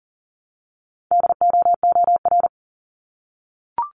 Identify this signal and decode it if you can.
Morse code, heard as a keyed tone, about 39 wpm, 700 Hz. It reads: DOOR
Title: Multi-letter Phonemes Repeated 3x 40wpm